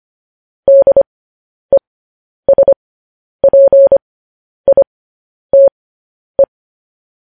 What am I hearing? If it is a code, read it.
Morse code, heard as a keyed tone, about 25 wpm, spaced as for 10 wpm, 550 Hz. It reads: DESPITE
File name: podcast / News headlines-1x-25x10